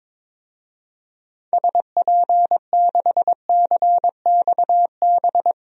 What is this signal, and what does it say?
Morse code, heard as a keyed tone, about 22 wpm, 700 Hz. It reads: SP6CXB
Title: DXCC-100 Call Signs - 6 Characters - 22wpm